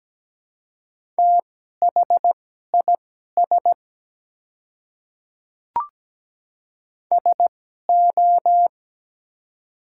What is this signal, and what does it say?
Morse code, heard as a keyed tone, about 17 wpm, spaced as for 12 wpm, 700 Hz. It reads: THIS SO
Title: 2x Character Spacing - Top 200 Words - 17wpm